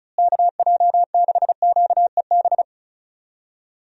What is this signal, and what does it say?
Morse code, heard as a keyed tone, about 34 wpm, 700 Hz. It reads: KJ6QEB
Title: USA Call Signs - 6 Characters - Repeated 2x - 35wpm